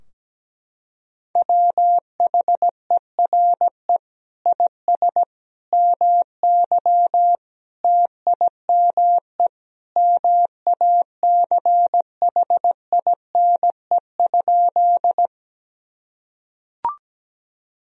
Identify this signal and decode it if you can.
Morse code, heard as a keyed tone, about 17 wpm, 700 Hz. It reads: WHERE IS MY TIME MACHINE?
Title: Sentences from Top 1000 Words - Review Part 2 - 17wpm